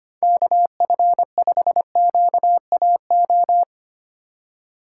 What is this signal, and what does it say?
Morse code, heard as a keyed tone, about 32 wpm, 700 Hz. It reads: KF5QAO